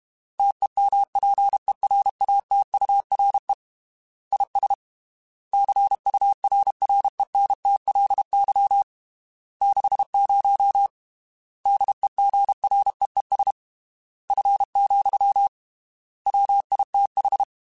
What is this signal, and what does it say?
Morse code, heard as a keyed tone, about 32 wpm, 800 Hz. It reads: TEMPERATURE IS CURRENTLY 60 DEGREES F, WITH
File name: Weather Sets / Weather Sets-mp3s / Weather 5 22 @30